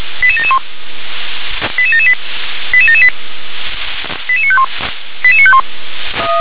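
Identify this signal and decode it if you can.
Five-tone selective call (ZVEI), heard as quick runs of five tones, tones of about 70 ms. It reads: 88771 88778 88778 88951 88951
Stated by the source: Five-tone ZVEI selective call received by SDR.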